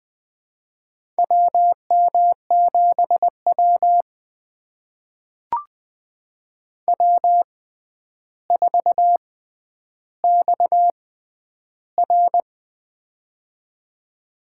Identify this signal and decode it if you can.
Morse code, heard as a keyed tone, about 21 wpm, spaced as for 10 wpm, 700 Hz. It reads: WM7W W4XR
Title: USA Call Signs - 4 Characters - 6x Character Spacing - 20wpm